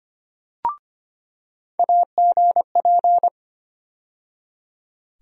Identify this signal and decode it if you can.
Morse code, heard as a keyed tone, about 25 wpm, 700 Hz. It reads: AGP
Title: USA Call Sign Postfixes - 3 Letters 25wpm